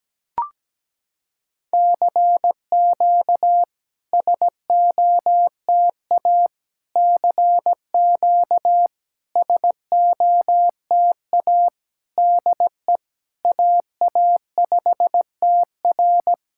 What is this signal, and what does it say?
Morse code, heard as a keyed tone, about 17 wpm, 700 Hz. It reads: CQ SOTA CQ SOTA DE AA5TR
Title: SOTA QSO Practice 17wpm